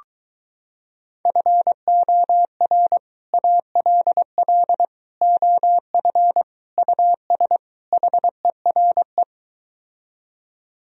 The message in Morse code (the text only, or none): FOR ALL OF US HERE